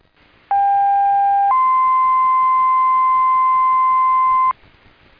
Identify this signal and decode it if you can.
Two-tone page, heard as a one-second tone, then a three-second tone, about 800 then 1040 Hz, with noise. It is heard from 0.5 to 4.5 s.